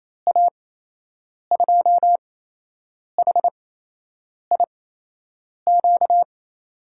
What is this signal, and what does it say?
Morse code, heard as a keyed tone, about 28 wpm, spaced as for 8 wpm, 700 Hz. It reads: A2HIQ
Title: USA Call Signs - 8x Character Spacing - 28wpm